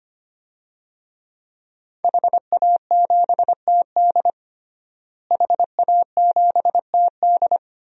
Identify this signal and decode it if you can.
Morse code, heard as a keyed tone, about 25 wpm, 700 Hz. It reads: HA7TD HA7TD
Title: DXCC-100 Call Signs - 5 Characters - Repeated 3x - 25wpm